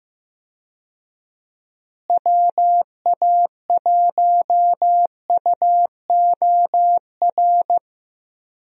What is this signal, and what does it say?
Morse code, heard as a keyed tone, about 15 wpm, 700 Hz. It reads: WA1UOR
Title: USA Call Signs - 6 Characters - Repeated 3x - 15wpm